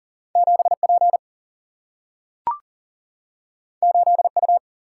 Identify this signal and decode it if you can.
Morse code, heard as a keyed tone, about 42 wpm, 700 Hz. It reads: 7P 8U
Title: Single-Letter Single-Number 40wpm